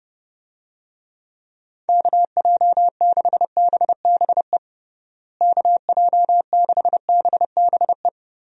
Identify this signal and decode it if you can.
Morse code, heard as a keyed tone, about 30 wpm, 700 Hz. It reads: KJ6BBE KJ6BBE
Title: USA Call Signs - 6 Characters - Repeated 3x - 30wpm